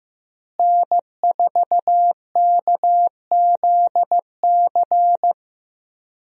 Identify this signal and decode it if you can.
Morse code, heard as a keyed tone, about 15 wpm, 700 Hz. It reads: N4KZC
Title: USA Call Signs - 5 Characters - Repeated 3x - 15wpm